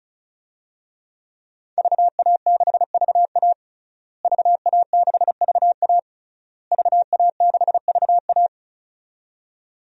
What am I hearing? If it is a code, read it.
Morse code, heard as a keyed tone, about 36 wpm, 700 Hz. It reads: VA6VA VA6VA VA6VA
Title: DXCC-100 Call Signs - 5 Characters - Repeated 3x - 35wpm